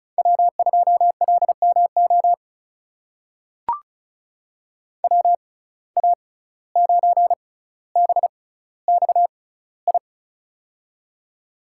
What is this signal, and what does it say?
Morse code, heard as a keyed tone, about 36 wpm, spaced as for 17 wpm, 700 Hz. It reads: W2LMO WA9BXI
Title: USA Call Signs - 6x Character Spacing - 35wpm